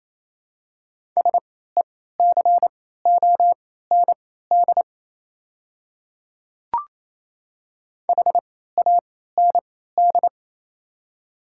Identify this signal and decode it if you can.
Morse code, heard as a keyed tone, about 27 wpm, spaced as for 16 wpm, 700 Hz. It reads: SECOND HAND
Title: 3x Character Spacing - Top 300 Words - 28wpm